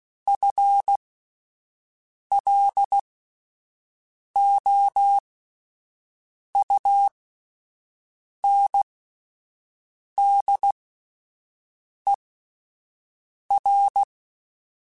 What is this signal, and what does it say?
Morse code, heard as a keyed tone, about 16 wpm, spaced as for 5 wpm, 800 Hz. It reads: FLOUNDER
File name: Spaced / FL Runs sp mp3s / FL Runs sp @10